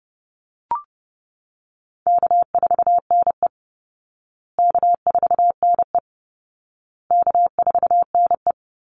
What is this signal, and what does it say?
Morse code, heard as a keyed tone, about 30 wpm, 700 Hz. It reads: K4NE K4NE K4NE
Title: Easy Call Signs Repeated 3x - 30wpm